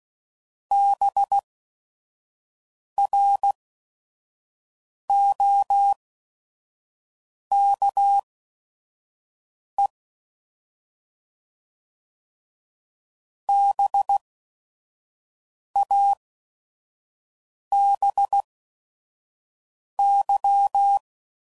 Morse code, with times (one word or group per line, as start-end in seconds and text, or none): 0.7-9.9 BROKE
13.5-21.0 BABY